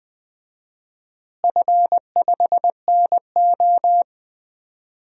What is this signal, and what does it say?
Morse code, heard as a keyed tone, about 20 wpm, 700 Hz. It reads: F5NO
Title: DXCC-100 Call Signs - 4 Characters - Repeated 2x - 20wpm